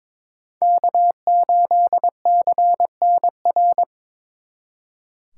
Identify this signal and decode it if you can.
Morse code, heard as a keyed tone, about 26 wpm, 700 Hz. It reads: K8CNR
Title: Speed Racing - USA 1x3 Call Signs 22wpm